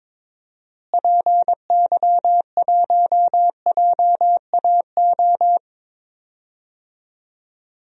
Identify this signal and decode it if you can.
Morse code, heard as a keyed tone, about 22 wpm, 700 Hz. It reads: PY1JAO
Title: DXCC-100 Call Signs - 6 Characters - Repeated 2x - 22wpm